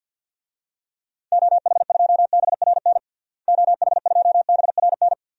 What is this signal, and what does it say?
Morse code, heard as a keyed tone, about 49 wpm, 700 Hz. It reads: OH2BRN OH2BRN
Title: DXCC-100 Call Signs - 6 Characters - Repeated 3x - 50wpm